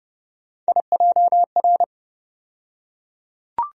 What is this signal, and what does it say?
Morse code, heard as a keyed tone, about 30 wpm, 700 Hz. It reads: IJR
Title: USA Call Sign Postfixes - 3 Letters 30wpm